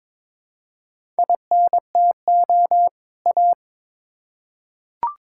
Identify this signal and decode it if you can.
Morse code, heard as a keyed tone, about 22 wpm, 700 Hz. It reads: INTO A
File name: Build A Phrase - 2-Word Phrases - 22wpm